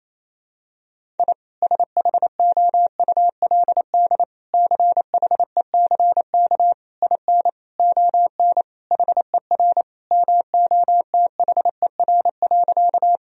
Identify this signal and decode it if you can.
Morse code, heard as a keyed tone, about 27 wpm, 700 Hz. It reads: I SHOULD CHECK IN ON HER MOTHER.
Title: Sentences from Top 1000 Words - Review Part 3 - 28wpm